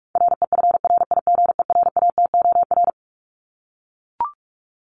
Morse code, heard as a keyed tone, about 44 wpm, 700 Hz. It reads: REFRIGERATOR
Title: Top 1501-1750 Words 45wpm